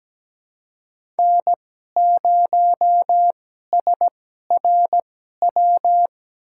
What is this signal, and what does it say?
Morse code, heard as a keyed tone, about 17 wpm, spaced as for 12 wpm, 700 Hz. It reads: N0SRW
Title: USA Call Signs - 5 Characters - 2x Character Spacing - 17wpm